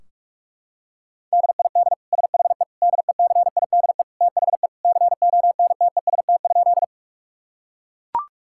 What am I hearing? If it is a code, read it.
Morse code, heard as a keyed tone, about 41 wpm, 700 Hz. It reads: DID SHE DECIDE THE CONTEST?
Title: Sentences from Top 1600 Words 45wpm